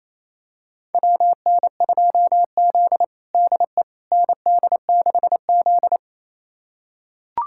Morse code, heard as a keyed tone, about 28 wpm, 700 Hz. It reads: WN2Z DE ND6Z
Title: USA Callsign DE Callsign - 4 Characters - 28wpm